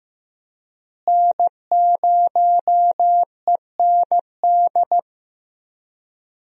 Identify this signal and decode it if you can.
Morse code, heard as a keyed tone, about 15 wpm, 700 Hz. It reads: N0END
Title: USA Call Signs - 5 Characters - Repeated 3x - 15wpm